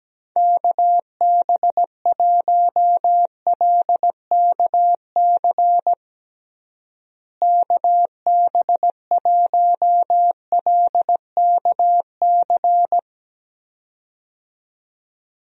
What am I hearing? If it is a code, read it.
Morse code, heard as a keyed tone, about 17 wpm, 700 Hz. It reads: KB1LKC KB1LKC
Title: USA Call Signs - 6 Characters - Repeated 2x - 17wpm